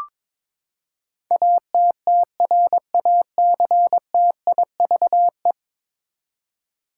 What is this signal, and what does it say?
Morse code, heard as a keyed tone, about 21 wpm, 700 Hz. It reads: ATTRACTIVE
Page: morse-code-ninja-practice-22wpm